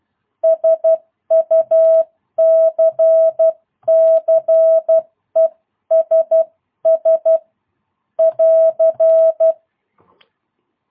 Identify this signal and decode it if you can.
Morse code, heard as a keyed tone, about 12 wpm, 650 Hz. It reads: SUCCESS <AR>
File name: jh8cbh / 2022cw